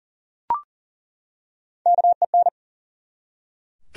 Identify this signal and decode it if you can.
Morse code, heard as a keyed tone, about 40 wpm, 700 Hz. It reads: KEN
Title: USA Common Names - Max 3 Letters - 40wpm